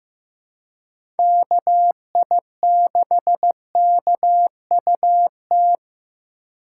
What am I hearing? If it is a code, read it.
Morse code, heard as a keyed tone, about 15 wpm, 700 Hz. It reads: KI6KUT
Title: USA Call Signs - 6 Characters - 2x Character Spacing - 15wpm